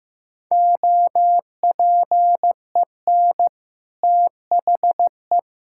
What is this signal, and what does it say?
Morse code, heard as a keyed tone, about 15 wpm, 700 Hz. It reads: OPEN THE
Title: Sentences from Top 1600 Words 15wpm